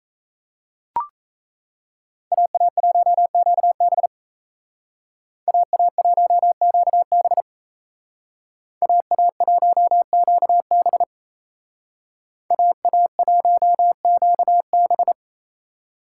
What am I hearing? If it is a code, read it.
Morse code, heard as a keyed tone, about 36 wpm, 700 Hz. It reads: AA1QB AA1QB AA1QB AA1QB